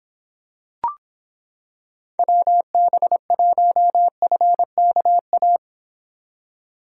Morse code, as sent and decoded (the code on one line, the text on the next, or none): .-- -... .---- ..-. -.- .-
WB1FKA